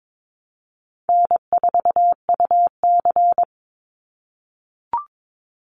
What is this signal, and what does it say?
Morse code, heard as a keyed tone, about 22 wpm, 700 Hz. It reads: N4UC
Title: Call Signs - 22wpm